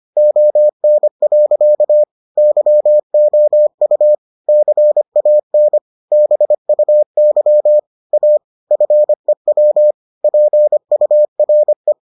Morse code, heard as a keyed tone, about 25 wpm, 600 Hz. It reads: ON. YOU CAN BUY A FEW PURE